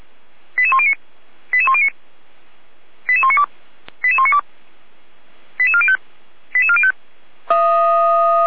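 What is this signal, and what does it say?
Five-tone selective call (ZVEI), heard as quick runs of five tones, tones of about 70 ms. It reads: 88109 88109 88192 88192 88405 88405
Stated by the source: ZVEI five-tone call 88103 received by SDR, recorded by feuerwehr-orlamuende (f85360) fire brigade alerting server.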